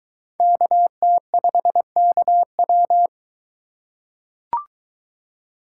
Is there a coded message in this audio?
Yes: Morse code from 0.4 to 3.1 s, heard as a keyed tone, about 21 wpm, 700 Hz. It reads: KT5KW